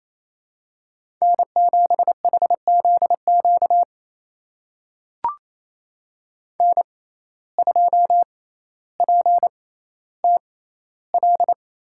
Morse code, heard as a keyed tone, about 28 wpm, spaced as for 14 wpm, 700 Hz. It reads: N7HZQ N2PTL
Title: USA Call Signs - 5 Characters - 6x Character Spacing - 28wpm